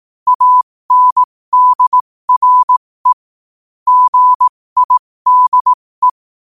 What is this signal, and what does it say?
Morse code, heard as a keyed tone, about 18 wpm, spaced as for 15 wpm, 1000 Hz. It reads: ANDRE GIDE
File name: qotdfiles / 15wpm